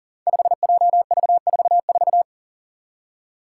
Morse code, heard as a keyed tone, about 40 wpm, 700 Hz. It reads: 5JV44
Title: Five Challenging Letter-Number Combinations 40wpm